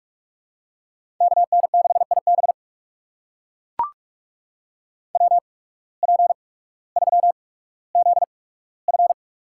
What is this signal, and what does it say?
Morse code, heard as a keyed tone, about 42 wpm, spaced as for 19 wpm, 700 Hz. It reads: KN6IB WP3ZF
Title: USA Call Signs - 5 Characters - 8x Character Spacing - 45wpm